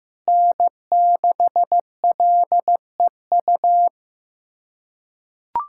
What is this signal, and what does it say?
Morse code, heard as a keyed tone, about 15 wpm, 700 Hz. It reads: N6LEU
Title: USA Callsign DE Callsign - 5 Characters - 15wpm